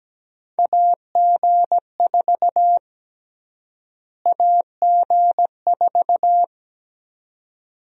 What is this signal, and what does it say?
Morse code, heard as a keyed tone, about 17 wpm, 700 Hz. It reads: AG4 AG4
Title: USA Call Sign Prefixes - Repeated 2x - 17wpm